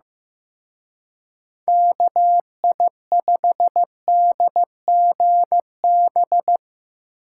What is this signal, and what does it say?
Morse code, heard as a keyed tone, about 16 wpm, 700 Hz. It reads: KI5DGB
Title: Speed Racing - USA 2x3 Call Signs 15wpm